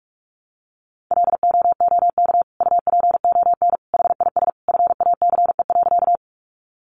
Morse code, heard as a keyed tone, about 44 wpm, 700 Hz. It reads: LOOK UPON HIS FACE.
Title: Sentences from Top 1100 Words 45wpm